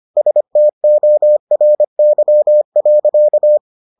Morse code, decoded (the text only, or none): STORY.